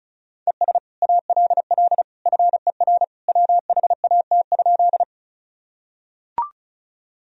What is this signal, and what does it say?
Morse code, heard as a keyed tone, about 36 wpm, 700 Hz. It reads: ES ALL FER WHAT?
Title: Sentences from Top 100 Words and Abbreviations 35wpm